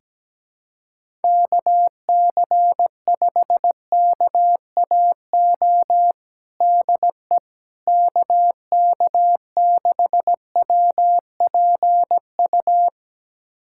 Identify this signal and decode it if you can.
Morse code, heard as a keyed tone, about 17 wpm, 700 Hz. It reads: KC5KAO DE KK6WPU
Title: USA Callsign DE Callsign 17wpm